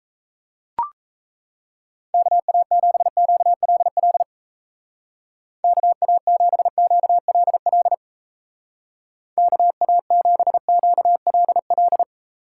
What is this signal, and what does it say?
Morse code, heard as a keyed tone, about 36 wpm, 700 Hz. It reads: KA7QLL KA7QLL KA7QLL